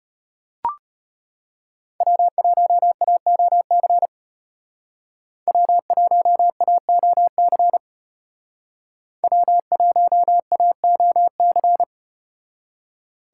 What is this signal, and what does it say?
Morse code, heard as a keyed tone, about 32 wpm, 700 Hz. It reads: W1AOC W1AOC W1AOC